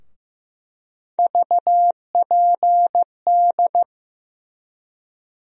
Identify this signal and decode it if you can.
Morse code, heard as a keyed tone, about 15 wpm, 700 Hz. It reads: VPD
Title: USA Call Sign Postfixes - 3 Letters 15wpm